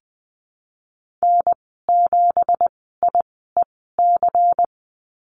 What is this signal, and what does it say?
Morse code, heard as a keyed tone, about 20 wpm, spaced as for 15 wpm, 700 Hz. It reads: N7IEC
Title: 173 Lesson 28g - Number 9 - Call Signs 20wpm